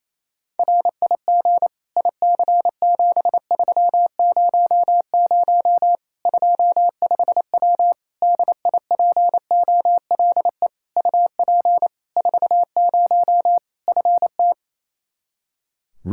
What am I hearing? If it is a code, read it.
Morse code, heard as a keyed tone, about 28 wpm, 700 Hz. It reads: RIG IC7300 25W DIPOLE UP 40 FT
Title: K1USN SST QSO Practice 28wpm